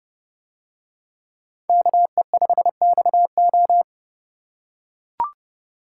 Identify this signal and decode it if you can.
Morse code, heard as a keyed tone, about 30 wpm, 700 Hz. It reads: KE5XO